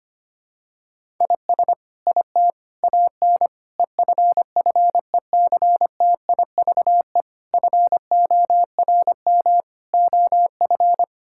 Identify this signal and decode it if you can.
Morse code, heard as a keyed tone, about 25 wpm, 700 Hz. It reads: IS IT AN EFFECTIVE FORM OF
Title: Sentences from Top 1000 Words - Review Part 6 - 25wpm